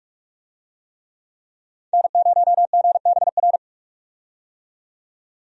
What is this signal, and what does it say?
Morse code, heard as a keyed tone, about 48 wpm, 700 Hz. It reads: N0GBR
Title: USA Call Signs - 5 Characters - 45wpm